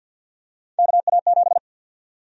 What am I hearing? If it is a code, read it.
Morse code, heard as a keyed tone, about 50 wpm, 700 Hz. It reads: KA7
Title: USA Call Sign Prefixes - 50wpm